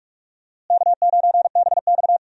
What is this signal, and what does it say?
Morse code, heard as a keyed tone, about 46 wpm, 700 Hz. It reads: K9BX
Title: USA Call Signs - 4 Characters - Repeated 2x - 45wpm